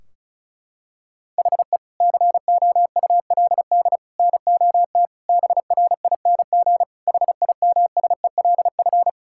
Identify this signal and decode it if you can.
Morse code, heard as a keyed tone, about 34 wpm, 700 Hz. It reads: HE COULD NOT BRING HIMSELF
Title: Sentences from Top 1600 Words 35wpm